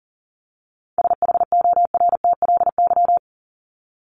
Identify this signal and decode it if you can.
Morse code, heard as a keyed tone, about 40 wpm, 700 Hz. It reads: SHORTLY
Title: LY Suffix - Top 100 Words with Base Words 40wpm